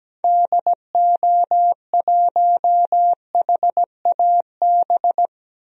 Morse code, heard as a keyed tone, about 17 wpm, 700 Hz. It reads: DO1HAB
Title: DXCC-100 Call Signs - 6 Characters - 17wpm